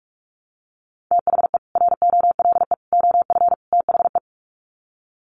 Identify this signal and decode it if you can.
Morse code, heard as a keyed tone, about 44 wpm, 700 Hz. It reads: THE ROLE OF THE
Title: Sets of 4 Words 45wpm